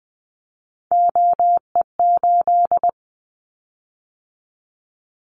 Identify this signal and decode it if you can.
Morse code, heard as a keyed tone, about 20 wpm, 700 Hz. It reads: OE8
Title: Callsign Prefixes for Top 100 DXCC - 20wpm